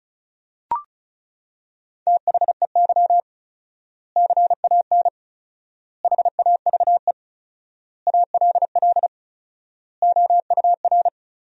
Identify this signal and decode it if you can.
Morse code, heard as a keyed tone, about 34 wpm, 700 Hz. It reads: THEY CAN HAVE ALL OUR
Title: 4x Word-Spacing - Sentences from Top 400 Words - 35wpm